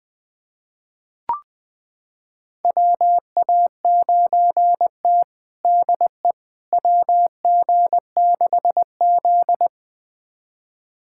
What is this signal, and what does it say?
Morse code, heard as a keyed tone, about 20 wpm, 700 Hz. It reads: WA9T DE WG6Z
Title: USA Callsign DE Callsign - 4 Characters - 20wpm